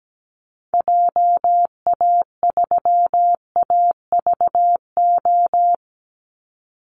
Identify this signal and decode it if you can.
Morse code, heard as a keyed tone, about 17 wpm, 700 Hz. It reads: JA3AVO
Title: Call Signs - 17wpm